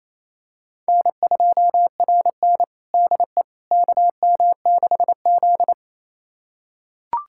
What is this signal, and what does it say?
Morse code, heard as a keyed tone, about 28 wpm, 700 Hz. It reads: N2RN DE KM6Z
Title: USA Callsign DE Callsign - 4 Characters - 28wpm